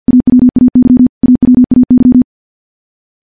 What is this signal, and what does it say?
Morse code, heard as a keyed tone, about 50 wpm, 250 Hz. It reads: AWAY AWAY
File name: morse / morse-training-50WPM